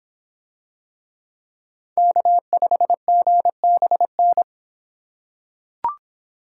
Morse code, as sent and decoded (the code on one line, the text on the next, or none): -.- ..... --. -... -.
K5GBN